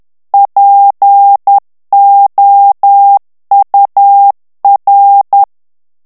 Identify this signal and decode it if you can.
Morse code, heard as a keyed tone, about 10 wpm, 800 Hz. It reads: POUR
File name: morse mp3 / FRANCAIS TEXTE CLAIR 2 VITESSE 10 MOTS MINUTES